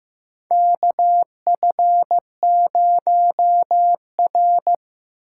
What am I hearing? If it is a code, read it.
Morse code, heard as a keyed tone, about 15 wpm, 700 Hz. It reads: KF0R
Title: USA Call Signs - 4 Characters - 4x Character Spacing - 15wpm